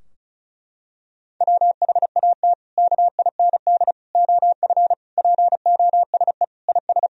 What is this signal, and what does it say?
Morse code, heard as a keyed tone, about 36 wpm, 700 Hz. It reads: WHAT KIND OF POSE IS
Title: Sentences from Top 1000 Words - Review Part 2 - 35wpm